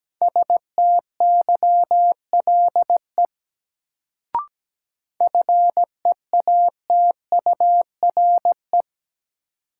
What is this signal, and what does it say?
Morse code, heard as a keyed tone, about 17 wpm, 700 Hz. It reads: STYLE FEATURE
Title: Sentences from Top 800 Words - New Word Drill - 17wpm